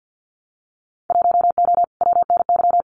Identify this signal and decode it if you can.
Morse code, heard as a keyed tone, about 50 wpm, 700 Hz. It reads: 1O WNY
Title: Winter Field Day Exchanges Encore 50wpm